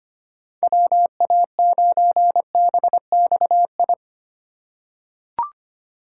Morse code, heard as a keyed tone, about 25 wpm, 700 Hz. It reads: WA9BXI
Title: USA Call Signs - 6 Characters - 4x Character Spacing - 25wpm